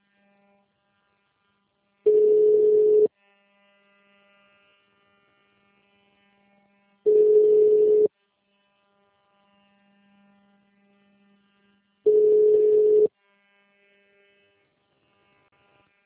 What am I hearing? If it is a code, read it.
Ringback tone (Europe, 425 Hz).